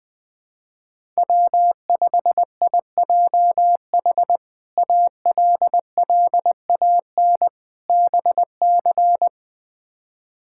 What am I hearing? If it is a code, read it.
Morse code, heard as a keyed tone, about 20 wpm, 700 Hz. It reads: W5IJH ALLAN BC
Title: K1USN SST QSO Practice - 20wpm